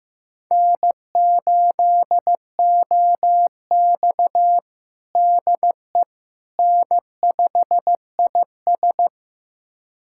Morse code, heard as a keyed tone, about 15 wpm, 700 Hz. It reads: N8OX DE N5IS
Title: USA Callsign DE Callsign - 4 Characters - 15wpm